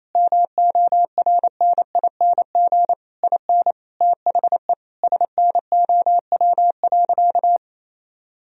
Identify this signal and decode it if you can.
Morse code, heard as a keyed tone, about 27 wpm, 700 Hz. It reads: MORNING IN THE SNOW.
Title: Sentences from Top 1000 Words - Review Part 3 - 28wpm